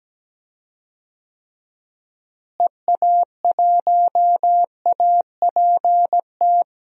Morse code, heard as a keyed tone, about 17 wpm, 700 Hz. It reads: EA1APT
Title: DXCC-100 Call Signs - 6 Characters - Repeated 3x - 17wpm